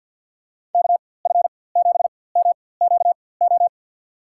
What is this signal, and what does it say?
Morse code, heard as a keyed tone, about 50 wpm, spaced as for 23 wpm, 700 Hz. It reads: KF7MQO
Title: USA Call Signs - 6 Characters - 4x Character Spacing - 50wpm